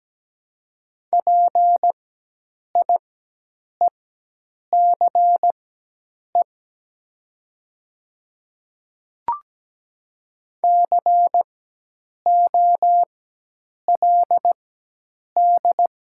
Morse code, heard as a keyed tone, about 17 wpm, spaced as for 8 wpm, 700 Hz. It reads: PIECE COLD
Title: 4x Character Spacing - Top 500 Words - 17wpm